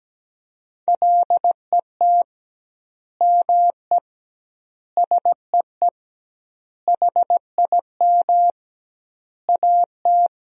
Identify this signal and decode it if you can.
Morse code, heard as a keyed tone, about 17 wpm, 700 Hz. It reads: LET ME SEE HIM AT
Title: 2x Word-Spacing - Sentences from Top 300 Words - 17wpm